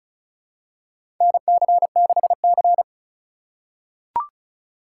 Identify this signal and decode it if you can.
Morse code, heard as a keyed tone, about 36 wpm, 700 Hz. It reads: NC6C
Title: USA Call Signs - 4 Characters - 2x Character Spacing - 35wpm